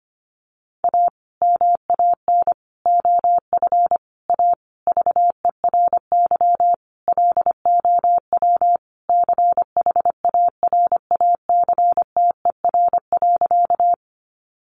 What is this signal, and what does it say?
Morse code, heard as a keyed tone, about 25 wpm, 700 Hz. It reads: A MAN OF A VERY LOW CHARACTER.
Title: Sentences from Top 1200 Words 25wpm